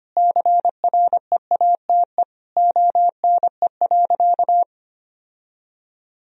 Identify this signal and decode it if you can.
Morse code, heard as a keyed tone, about 25 wpm, 700 Hz. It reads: CREATE ONE.
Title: Sentences from Top 1000 Words - Review Part 3 - 25wpm